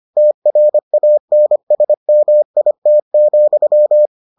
Morse code, heard as a keyed tone, about 25 wpm, 600 Hz. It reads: TRANSMIT,